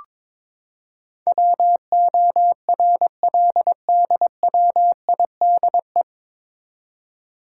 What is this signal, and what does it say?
Morse code, heard as a keyed tone, about 22 wpm, 700 Hz. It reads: WORLDWIDE
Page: Morse Code Ninja practice 22wpm